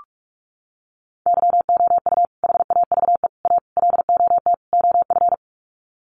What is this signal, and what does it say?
Morse code, heard as a keyed tone, about 44 wpm, 700 Hz. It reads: YOU HAVE A LOT OF
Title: Sets of 5 Words Encore 45wpm